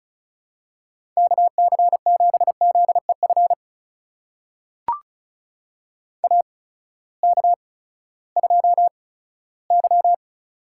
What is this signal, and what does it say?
Morse code, heard as a keyed tone, about 36 wpm, spaced as for 18 wpm, 700 Hz. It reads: KC7ZEF AK2Y
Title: USA Call Signs - 8x Character Spacing - 35wpm